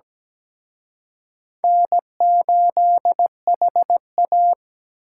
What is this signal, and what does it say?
Morse code, heard as a keyed tone, about 20 wpm, 700 Hz. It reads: N8HA
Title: Speed Racing - USA 1x2 Call Signs 17wpm